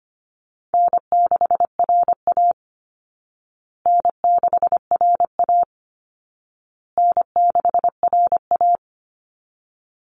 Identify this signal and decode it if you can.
Morse code, heard as a keyed tone, about 25 wpm, 700 Hz. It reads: N6RA N6RA N6RA
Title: Easy Call Signs Repeated 3x - 25wpm